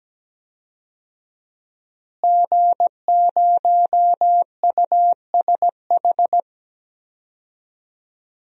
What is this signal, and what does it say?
Morse code, heard as a keyed tone, about 17 wpm, 700 Hz. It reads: G0USH
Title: DXCC-100 Call Signs - 5 Characters - Repeated 2x - 17wpm